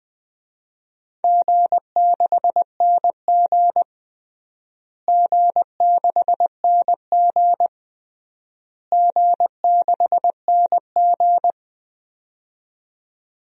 Morse code, heard as a keyed tone, about 21 wpm, 700 Hz. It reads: G6NG G6NG G6NG
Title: DXCC-100 Call Signs - 4 Characters - Repeated 3x - 20wpm